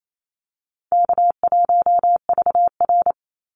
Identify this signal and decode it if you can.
Morse code, heard as a keyed tone, about 28 wpm, 700 Hz. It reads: K1VR
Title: Call Signs Repeated 2x - 28wpm